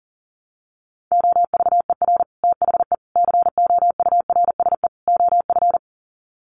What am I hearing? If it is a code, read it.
Morse code, heard as a keyed tone, about 40 wpm, 700 Hz. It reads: OVER THE COURSE OF
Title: Sets of 4 Words 40wpm